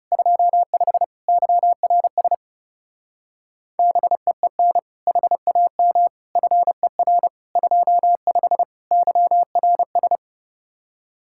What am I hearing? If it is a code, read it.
Morse code, heard as a keyed tone, about 36 wpm, 700 Hz. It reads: 25 YRS BEEN HAM FER 25 YRS
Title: Speed-Racing Common QSO Phrases Encore - 30wpm